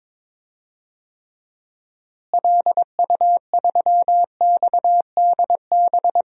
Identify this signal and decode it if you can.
Morse code, heard as a keyed tone, about 21 wpm, 700 Hz. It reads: LU3XDB
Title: DXCC-100 Call Signs - 6 Characters - Repeated 3x - 22wpm